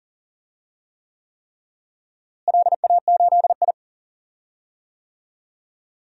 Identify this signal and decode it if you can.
Morse code, heard as a keyed tone, about 40 wpm, 700 Hz. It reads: LA8I
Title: DXCC-100 Call Signs - 4 Characters - Repeated 3x - 40wpm